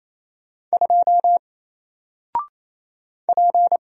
Morse code, heard as a keyed tone, about 28 wpm, 700 Hz. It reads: 2 P
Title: Single Letter-Number Alt 28wpm